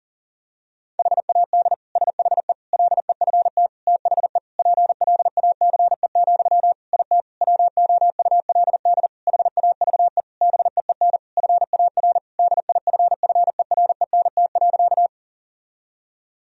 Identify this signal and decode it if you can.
Morse code, heard as a keyed tone, about 40 wpm, 700 Hz. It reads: HAD SHE LEFT THE PLACE, IT WOULD HAVE BEEN FAR DIFFERENT.
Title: Sentences from Top 1000 Words - Review Part 3 - 40wpm